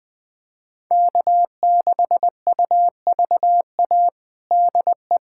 Morse code, heard as a keyed tone, about 20 wpm, 700 Hz. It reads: K6UVA DE
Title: USA Callsign DE Callsign - 5 Characters - 20wpm